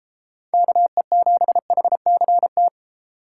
Morse code, heard as a keyed tone, about 33 wpm, 700 Hz. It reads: KE7HCT
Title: Speed Racing - USA 2x3 Call Signs 28wpm